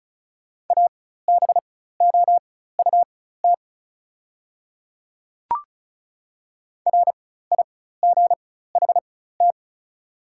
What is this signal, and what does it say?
Morse code, heard as a keyed tone, about 34 wpm, spaced as for 16 wpm, 690 Hz. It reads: ABOUT RIGHT
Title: 4x Character Spacing - Top 500 Words - 35wpm